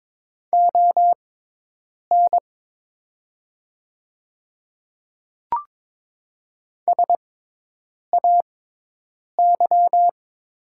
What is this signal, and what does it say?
Morse code, heard as a keyed tone, about 22 wpm, spaced as for 8 wpm, 700 Hz. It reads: ON SAY
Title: 6x Character Spacing - Top 200 Words - 22wpm